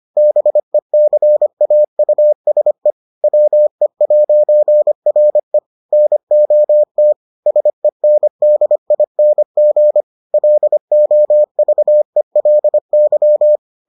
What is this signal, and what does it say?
Morse code, heard as a keyed tone, about 25 wpm, 600 Hz. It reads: BECAUSE WE'RE NOT SENDING LOVELY